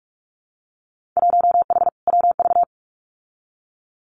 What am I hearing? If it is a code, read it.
Morse code, heard as a keyed tone, about 46 wpm, 700 Hz. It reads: 1H WV
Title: Winter Field Day Exchanges 45wpm